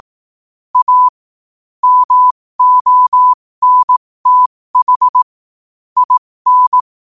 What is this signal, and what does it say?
Morse code, heard as a keyed tone, about 18 wpm, spaced as for 15 wpm, 1000 Hz. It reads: A MONTH IN
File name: qotdfiles / 15wpm